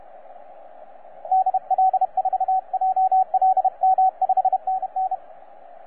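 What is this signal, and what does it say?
Morse code, heard as a keyed tone, about 31 wpm, 760 Hz, with radio noise. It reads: DL4JLM 5NN
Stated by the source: Morse on 20 m CW from 9M2/OK1CZ on Babi Besar island, AS-046.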